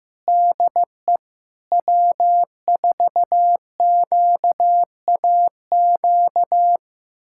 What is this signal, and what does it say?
Morse code, heard as a keyed tone, about 16 wpm, 700 Hz. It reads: DE W4QAQ
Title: USA Callsign DE Callsign - 5 Characters - 15wpm